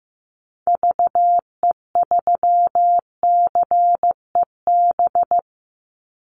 Morse code, heard as a keyed tone, about 15 wpm, 700 Hz. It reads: VE3CEB
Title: Call Signs Repeated 2x - 15wpm